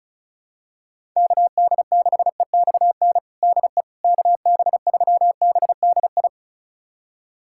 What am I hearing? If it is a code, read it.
Morse code, heard as a keyed tone, about 36 wpm, 700 Hz. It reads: KD6EXN DE KB3BDI
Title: USA Callsign DE Callsign 35wpm